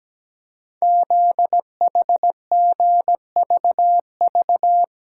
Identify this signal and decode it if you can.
Morse code, heard as a keyed tone, about 17 wpm, 700 Hz. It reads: ZHGVV
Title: Code Groups - 5 Letters - 17wpm